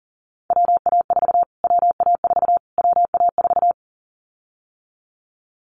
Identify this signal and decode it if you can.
Morse code, heard as a keyed tone, about 40 wpm, 700 Hz. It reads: WA4 WA4 WA4
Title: Callsign Prefixes for Top 25 DXCC - Repeated 3x - 40wpm